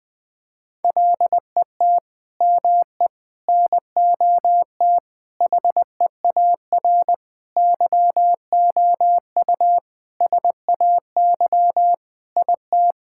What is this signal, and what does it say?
Morse code, heard as a keyed tone, about 20 wpm, 700 Hz. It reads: LET ME NOT HEAR YOU SAY IT
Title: Sentences from Top 1600 Words 20wpm